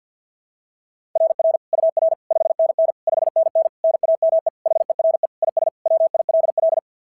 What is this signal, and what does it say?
Morse code, heard as a keyed tone, about 47 wpm, 690 Hz. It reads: RR UR 5NN 5NN NAME HERE IS WILL
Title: SOTA QSO Practice 50wpm